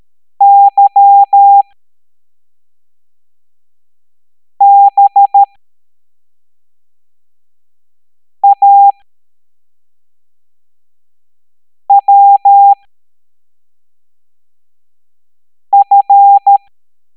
Morse code, heard as a keyed tone, about 13 wpm, spaced as for 3 wpm, 800 Hz. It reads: YBAWF
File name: hamdata / L09-EX04